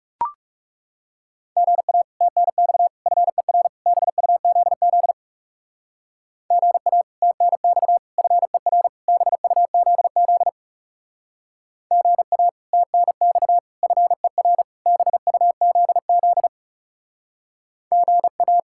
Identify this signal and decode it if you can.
Morse code, heard as a keyed tone, about 40 wpm, 700 Hz. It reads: GA TNX FER BUZZ GA TNX FER BUZZ GA TNX FER BUZZ GA